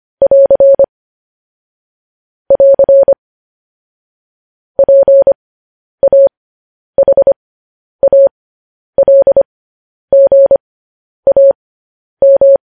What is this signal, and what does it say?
Morse code, heard as a keyed tone, about 25 wpm, spaced as for 10 wpm, 550 Hz. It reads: <AR> <AR> PAHALGAM
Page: news headlines in morse code at 10 wpm